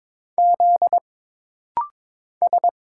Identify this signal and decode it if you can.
Morse code, heard as a keyed tone, about 22 wpm, 700 Hz. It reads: Z S
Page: This rapid-fire timing is geared to push your ICR (Instant Character Recognition) skills.